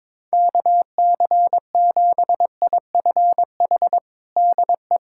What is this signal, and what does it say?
Morse code, heard as a keyed tone, about 22 wpm, 700 Hz. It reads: KC7IFH DE
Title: USA Callsign DE Callsign 22wpm